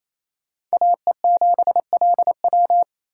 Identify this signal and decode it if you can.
Morse code, heard as a keyed tone, about 38 wpm, spaced as for 33 wpm, 700 Hz. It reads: AE7LW
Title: Speed Racing - USA 2x2 Call Signs 28wpm